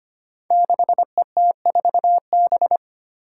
Morse code, heard as a keyed tone, about 25 wpm, 700 Hz. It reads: 6ET4B
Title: Five Challenging Letter-Number Combinations 25wpm